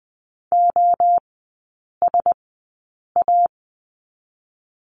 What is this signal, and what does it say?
Morse code, heard as a keyed tone, about 20 wpm, spaced as for 8 wpm, 700 Hz. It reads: OSA
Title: 060 Lesson 11e - Letter h - Three Characters 20wpm